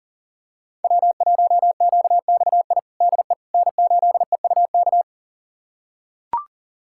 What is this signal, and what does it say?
Morse code, heard as a keyed tone, about 40 wpm, 700 Hz. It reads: W1QXI DE N8EUK
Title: USA Callsign DE Callsign - 5 Characters - 40wpm